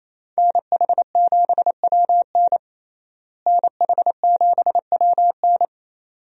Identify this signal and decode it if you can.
Morse code, heard as a keyed tone, about 28 wpm, 700 Hz. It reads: NH7WN NH7WN
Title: USA Call Signs - 5 Characters - Repeated 2x - 28wpm